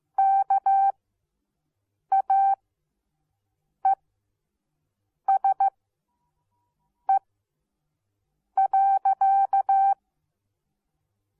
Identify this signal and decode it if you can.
Morse code, heard as a keyed tone, about 15 wpm, spaced as for 5 wpm, 780 Hz. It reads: KAESE.